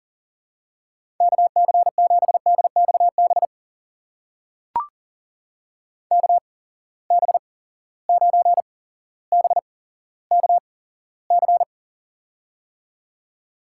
Morse code, heard as a keyed tone, about 40 wpm, spaced as for 17 wpm, 700 Hz. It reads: KC7DXB KB9BKC
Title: USA Call Signs - 6 Characters - 8x Character Spacing - 40wpm